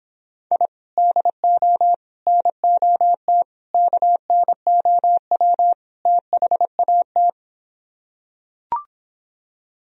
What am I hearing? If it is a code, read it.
Morse code, heard as a keyed tone, about 26 wpm, 700 Hz. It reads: I DO NOT KNOW THAT